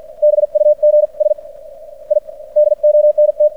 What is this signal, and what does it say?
Morse code, heard as a keyed tone, about 56 wpm, spaced as for 45 wpm, 600 Hz. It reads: BUGS I DONT
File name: archived / CRSnet